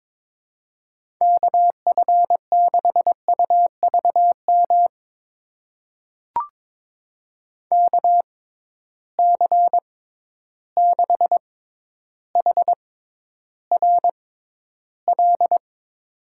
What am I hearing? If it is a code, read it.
Morse code, heard as a keyed tone, about 22 wpm, spaced as for 11 wpm, 700 Hz. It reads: KF6UVM KC6HRL
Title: USA Call Signs - 6 Characters - 6x Character Spacing - 22wpm